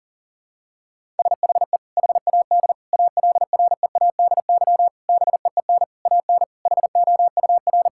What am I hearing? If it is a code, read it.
Morse code, heard as a keyed tone, about 40 wpm, 700 Hz. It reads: SHE HAD ALREADY BEEN AN HOUR
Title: Sentences from Top 1000 Words - Review Part 6 - 40wpm